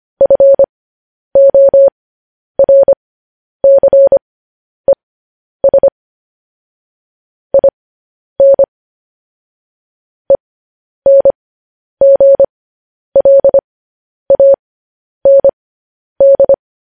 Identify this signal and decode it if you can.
Morse code, heard as a keyed tone, about 25 wpm, spaced as for 10 wpm, 550 Hz. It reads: FORCES IN ENGLAND